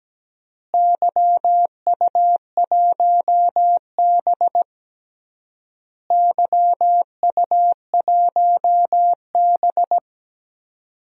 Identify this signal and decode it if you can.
Morse code, heard as a keyed tone, about 17 wpm, 700 Hz. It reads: YU1B YU1B
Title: DXCC-100 Call Signs - 4 Characters - Repeated 3x - 17wpm